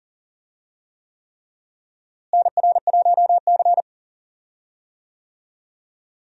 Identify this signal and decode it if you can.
Morse code, heard as a keyed tone, about 40 wpm, 700 Hz. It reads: NR1C
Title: USA Call Signs - 4 Characters - Repeated 3x - 40wpm